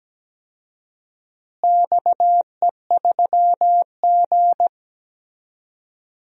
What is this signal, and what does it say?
Morse code, heard as a keyed tone, about 17 wpm, 700 Hz. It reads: XE3G
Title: DXCC-100 Call Signs - 4 Characters - Repeated 3x - 17wpm